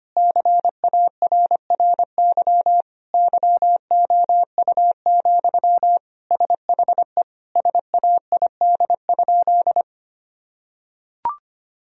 Morse code, heard as a keyed tone, about 25 wpm, 700 Hz. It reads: CARRY YOU, SHE SAID?
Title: Sentences from Top 1000 Words - Review Part 1 - 25wpm